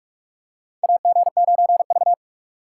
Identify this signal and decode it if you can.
Morse code, heard as a keyed tone, about 48 wpm, 700 Hz. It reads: AG9V
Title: USA Call Signs - 4 Characters - 6x Character Spacing - 45wpm